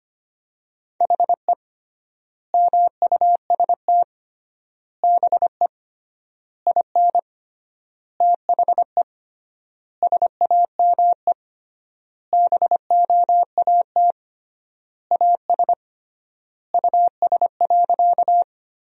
Morse code, heard as a keyed tone, about 25 wpm, 700 Hz. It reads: HE MUST BE IN THE SAME BOAT AS US.
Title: 3x Word-Spacing - Sentences from Top 500 Words - 25wpm